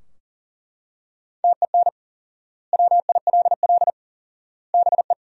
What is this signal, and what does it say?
Morse code, heard as a keyed tone, about 40 wpm, 700 Hz. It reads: TEN WILL BE
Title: 4x Word-Spacing - Sentences from Top 400 Words - 40wpm